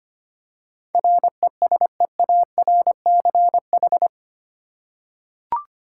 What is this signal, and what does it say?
Morse code, heard as a keyed tone, about 25 wpm, 700 Hz. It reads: RESEARCH
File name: Eight Letter Words - 25wpm